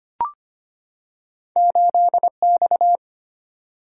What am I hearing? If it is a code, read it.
Morse code, heard as a keyed tone, about 25 wpm, 700 Hz. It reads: 8X
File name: Two Letter-Number Combinations - 25wpm